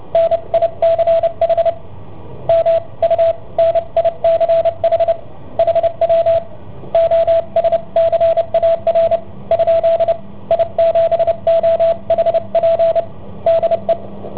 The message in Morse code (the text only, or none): NICH M U NICH HW OSCAR ? I7OHP DE